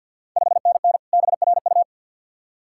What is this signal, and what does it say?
Morse code, heard as a keyed tone, about 53 wpm, 690 Hz. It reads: 5NN BRU
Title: Ontario QSO Party Exchange 50wpm